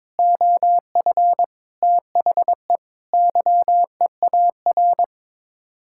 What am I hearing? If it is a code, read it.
Morse code, heard as a keyed tone, about 22 wpm, 700 Hz. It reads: OF THE YEAR
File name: practice / 22wpm / Sets of 8 Words - 22wpm